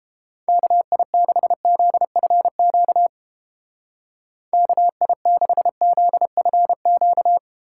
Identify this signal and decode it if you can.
Morse code, heard as a keyed tone, about 30 wpm, 700 Hz. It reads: KI6ZFQ KI6ZFQ